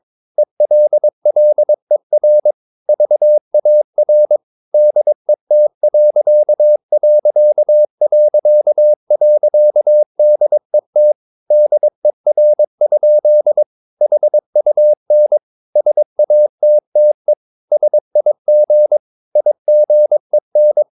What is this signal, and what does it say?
Morse code, heard as a keyed tone, about 22 wpm, 600 Hz. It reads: ELLER VAR DET....DET DER? HUN SATTE SIG IGEN
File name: Ferie QRS 22wpm